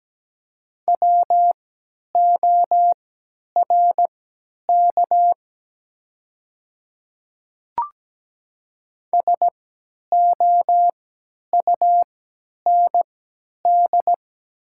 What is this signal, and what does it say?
Morse code, heard as a keyed tone, about 17 wpm, spaced as for 10 wpm, 700 Hz. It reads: WORK SOUND
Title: 3x Character Spacing - Top 200 Words - 17wpm